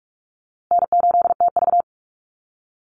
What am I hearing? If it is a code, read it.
Morse code, heard as a keyed tone, about 44 wpm, 700 Hz. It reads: N8TV
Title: USA Call Signs Repeated 3x 45wpm